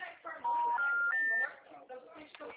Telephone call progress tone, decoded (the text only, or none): special information tone